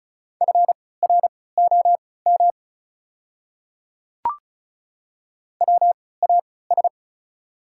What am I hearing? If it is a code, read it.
Morse code, heard as a keyed tone, about 36 wpm, spaced as for 20 wpm, 690 Hz. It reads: FROM WAS
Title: 3x Character Spacing - Top 100 Words - 35wpm